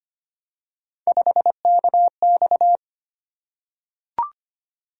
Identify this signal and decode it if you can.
Morse code, heard as a keyed tone, about 25 wpm, 700 Hz. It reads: 5KX
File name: Three Letter-Number Combinations - 25wpm